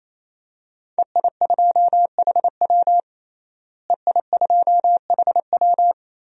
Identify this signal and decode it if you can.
Morse code, heard as a keyed tone, about 28 wpm, 700 Hz. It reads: EI2HW EI2HW
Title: DXCC-100 Call Signs - 5 Characters - Repeated 3x - 28wpm